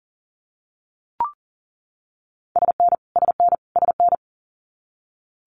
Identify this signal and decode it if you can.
Morse code, heard as a keyed tone, about 40 wpm, 700 Hz. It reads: SN SN SN
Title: Multi-letter Phonemes Repeated 3x 40wpm